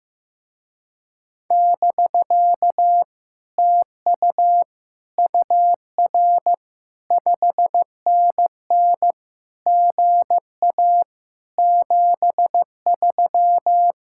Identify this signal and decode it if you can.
Morse code, heard as a keyed tone, about 15 wpm, 700 Hz. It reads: <BK> TU UR 5NN GA 73
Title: Speed-Racing Common QSO Phrases Encore - 15wpm